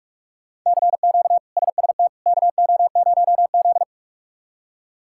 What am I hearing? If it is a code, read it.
Morse code, heard as a keyed tone, about 46 wpm, 700 Hz. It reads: CQ SST KO0Z
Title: K1USN SST QSO Practice - 45wpm